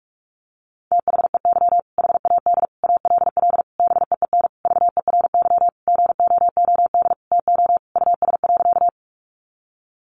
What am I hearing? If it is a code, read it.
Morse code, heard as a keyed tone, about 44 wpm, 700 Hz. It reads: THEY HAD ALL BEEN VERY GOOD TO US.
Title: Sentences from Top 200 Words 45wpm